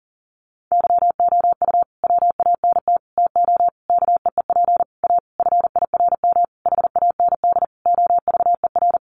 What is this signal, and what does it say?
Morse code, heard as a keyed tone, about 37 wpm, 700 Hz. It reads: YOU WANT TO KEEP A FIRM HAND OVER
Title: Sentences from Top 1400 Words 40wpm